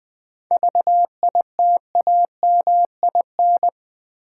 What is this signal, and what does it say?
Morse code, heard as a keyed tone, about 20 wpm, 700 Hz. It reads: VITAMIN
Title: 7 Letter Words - First 3 Letters Given - 20wpm